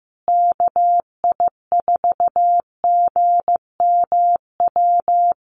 Morse code, heard as a keyed tone, about 15 wpm, 700 Hz. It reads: KI4GMW
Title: USA Call Signs Repeated 3x 15wpm